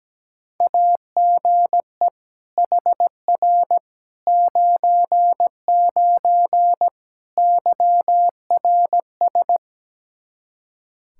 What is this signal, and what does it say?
Morse code, heard as a keyed tone, about 17 wpm, 700 Hz. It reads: AGE HR 99 YRS
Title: Practice QSO - Exchange 3 - More Info - 17wpm